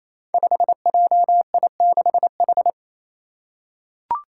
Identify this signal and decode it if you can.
Morse code, heard as a keyed tone, about 28 wpm, 700 Hz. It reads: 5JI6H
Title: Five Challenging Letter-Number Combinations 28wpm